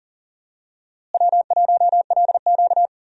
Morse code, heard as a keyed tone, about 40 wpm, 700 Hz. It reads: W1LQ
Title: USA Call Signs - 4 Characters - Repeated 3x - 40wpm